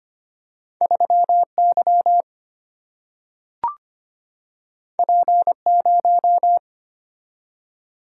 Morse code, heard as a keyed tone, about 25 wpm, 700 Hz. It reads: 3 Y P 0
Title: Single-Letter Single-Number 25wpm